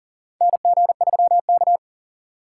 Morse code, heard as a keyed tone, about 38 wpm, 700 Hz. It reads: NG3K
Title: USA Call Signs - 4 Characters - 4x Character Spacing - 40wpm